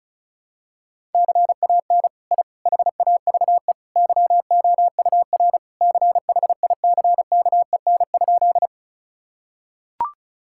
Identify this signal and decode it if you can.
Morse code, heard as a keyed tone, about 34 wpm, 700 Hz. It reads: CAN I HAVE YOUR CHICKEN?
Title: Sentences from Top 1000 Words - Review Part 5 - 35wpm